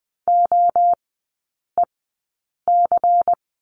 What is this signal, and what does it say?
Morse code, heard as a keyed tone, about 20 wpm, spaced as for 8 wpm, 690 Hz. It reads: OEC
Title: 133 Lesson 22e - Question Mark - Three Characters 20wpm